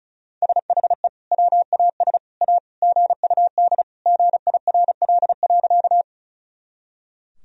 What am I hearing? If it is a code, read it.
Morse code, heard as a keyed tone, about 34 wpm, 700 Hz. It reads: SHE WAS A GUD GIRL.
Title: Sentences from Top 400 Words and Abbreviations 35wpm